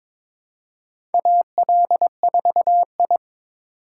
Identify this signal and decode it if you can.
Morse code, heard as a keyed tone, about 22 wpm, 700 Hz. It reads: AL4I
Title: USA Call Signs - 4 Characters - 2x Character Spacing - 22wpm